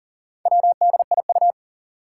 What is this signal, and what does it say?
Morse code, heard as a keyed tone, about 40 wpm, 700 Hz. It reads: WDIU
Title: Four Challenging Letter-Number Combinations 40wpm